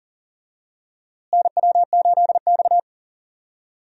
Morse code, heard as a keyed tone, about 40 wpm, 700 Hz. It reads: NW8X
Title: USA Call Signs - 4 Characters - Repeated 3x - 40wpm